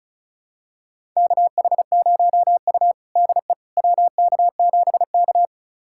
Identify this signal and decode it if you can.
Morse code, heard as a keyed tone, about 34 wpm, 700 Hz. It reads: KH0U DE WK7K
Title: USA Callsign DE Callsign - 4 Characters - 35wpm